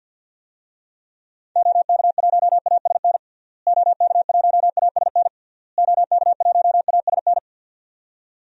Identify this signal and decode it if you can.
Morse code, heard as a keyed tone, about 51 wpm, 700 Hz. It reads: OK1ASN OK1ASN OK1ASN
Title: DXCC-100 Call Signs - 6 Characters - Repeated 3x - 50wpm